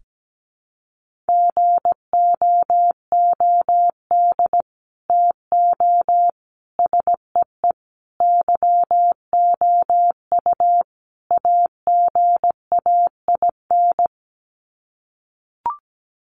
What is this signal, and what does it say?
Morse code, heard as a keyed tone, about 17 wpm, 700 Hz. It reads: GOOD TO SEE YOU AGAIN
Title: Sets of 5 Words Encore2 17wpm